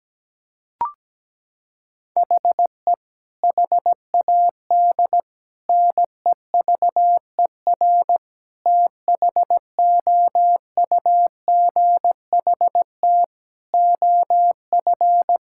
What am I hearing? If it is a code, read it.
Morse code, heard as a keyed tone, about 17 wpm, 700 Hz. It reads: HE HAD NEVER THOUGHT OF
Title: Sentences from Top 1600 Words 17wpm